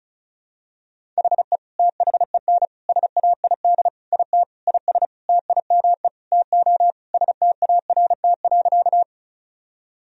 Morse code, heard as a keyed tone, about 36 wpm, 700 Hz. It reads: HE THEN SAID IT IS TIME TO START.
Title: Sentences from Top 1000 Words - Review Part 1 - 35wpm